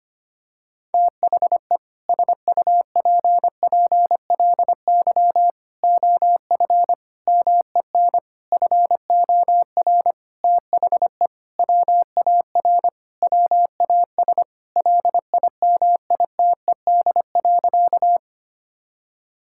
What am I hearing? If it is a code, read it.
Morse code, heard as a keyed tone, about 25 wpm, 700 Hz. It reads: THE SUPPLY OF MEN FOR THE WAR WAS LIMITED.
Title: Sentences from Top 1000 Words - Review Part 6 - 25wpm